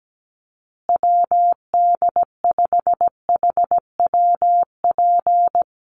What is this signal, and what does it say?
Morse code, heard as a keyed tone, about 17 wpm, 700 Hz. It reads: WD5HWP
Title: USA Call Signs Repeated 2x 17wpm